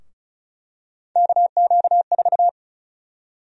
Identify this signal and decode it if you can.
Morse code, heard as a keyed tone, about 36 wpm, 700 Hz. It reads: KQ4
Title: USA Call Sign Prefixes - Repeated 2x - 35wpm